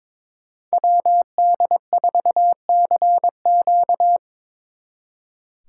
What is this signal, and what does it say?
Morse code, heard as a keyed tone, about 24 wpm, 700 Hz. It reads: WD4CQ
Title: Speed Racing - USA 2x2 Call Signs 22wpm